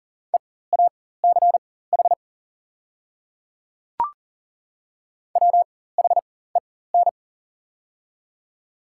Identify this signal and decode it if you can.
Morse code, heard as a keyed tone, about 40 wpm, spaced as for 19 wpm, 690 Hz. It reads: EACH WHEN
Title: 4x Character Spacing - Top 100 Words - 40wpm